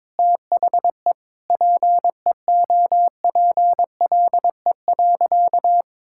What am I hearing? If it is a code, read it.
Morse code, heard as a keyed tone, about 22 wpm, 700 Hz. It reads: THE PEOPLE.
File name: practice / 22wpm / Sentences with 8-Word Phrases - 22wpm